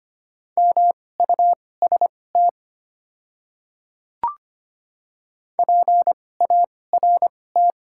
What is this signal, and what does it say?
Morse code, heard as a keyed tone, about 25 wpm, spaced as for 18 wpm, 700 Hz. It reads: MUST PART
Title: 2x Character Spacing - Top 200 Words - 25wpm